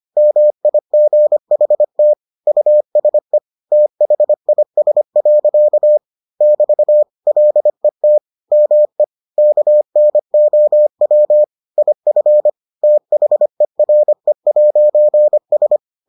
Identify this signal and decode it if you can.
Morse code, heard as a keyed tone, about 25 wpm, 600 Hz. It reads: MIGHT USE THIS. <BT> LET ME KNOW IF THERE'S